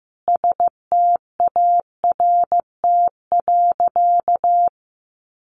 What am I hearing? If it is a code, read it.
Morse code, heard as a keyed tone, about 14 wpm, 700 Hz. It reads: START.
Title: Sentences from Top 700 Words 15wpm